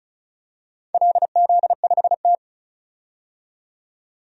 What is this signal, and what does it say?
Morse code, heard as a keyed tone, about 34 wpm, 700 Hz. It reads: LZ5T
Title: DXCC-100 Call Signs - 4 Characters - 35wpm